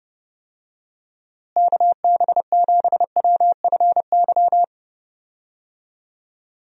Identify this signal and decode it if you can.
Morse code, heard as a keyed tone, about 32 wpm, 700 Hz. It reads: KB7WFY
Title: USA Call Signs - 6 Characters - Repeated 2x - 30wpm